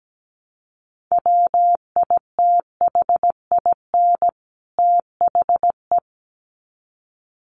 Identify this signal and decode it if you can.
Morse code, heard as a keyed tone, about 17 wpm, 700 Hz. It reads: WITHIN THE
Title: Sets of 2 Words 17wpm